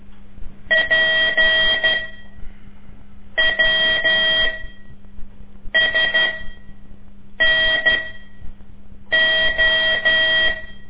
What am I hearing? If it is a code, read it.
Morse code, heard as a keyed tone, about 12 wpm, spaced as for 5 wpm, 1900 Hz. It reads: PWSNO
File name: ik1weg2 / CORSI CW / lez5-2